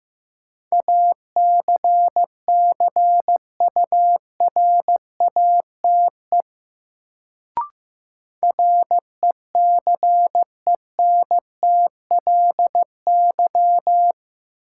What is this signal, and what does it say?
Morse code, heard as a keyed tone, about 15 wpm, 700 Hz. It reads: ACCURATE RECENTLY
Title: Sentences from Top 1200 Words - New Word Drill - 15wpm